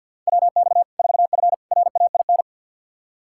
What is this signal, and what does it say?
Morse code, heard as a keyed tone, about 50 wpm, 700 Hz. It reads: WX 4F RAIN
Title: Practice QSO - Exchange 2 - Info - 50wpm